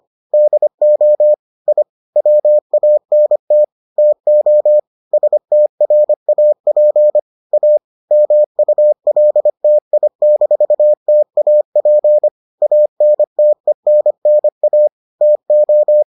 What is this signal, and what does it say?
Morse code, heard as a keyed tone, about 25 wpm, 600 Hz. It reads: DO I WANT TO STRAP A MULTI-TAP ANTENNA TO